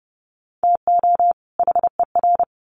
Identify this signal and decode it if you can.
Morse code, heard as a keyed tone, about 30 wpm, 700 Hz. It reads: TO HER
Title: Sets of 2 Words 30wpm